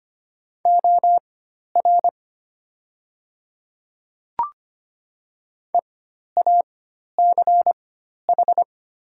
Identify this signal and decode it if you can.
Morse code, heard as a keyed tone, about 25 wpm, spaced as for 12 wpm, 700 Hz. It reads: OR EACH
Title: 4x Character Spacing - Top 100 Words - 25wpm